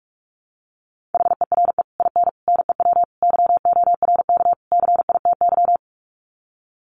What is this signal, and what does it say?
Morse code, heard as a keyed tone, about 44 wpm, 700 Hz. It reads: HERE IN NEW YORK CITY
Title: Sets of 5 Words Encore2 45wpm